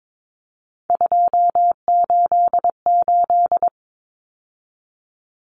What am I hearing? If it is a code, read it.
Morse code, heard as a keyed tone, about 22 wpm, 700 Hz. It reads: 288
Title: Common Cut Numbers - 3-Digit 22wpm